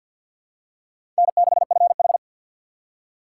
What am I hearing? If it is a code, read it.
Morse code, heard as a keyed tone, about 52 wpm, 700 Hz. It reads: N6FH
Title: USA Call Signs - 4 Characters - 3x Character Spacing - 50wpm